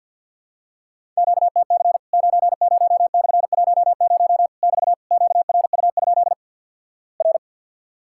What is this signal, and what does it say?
Morse code, heard as a keyed tone, about 54 wpm, 700 Hz. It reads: QTC 90/10 <BT> QRU? R
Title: Worked All Europe DX Contest - QTC Exchange with Repeats - 50wpm